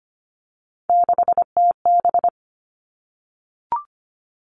Morse code, heard as a keyed tone, about 25 wpm, 700 Hz. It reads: 6TB
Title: All Cut Numbers - 3-Digit 25wpm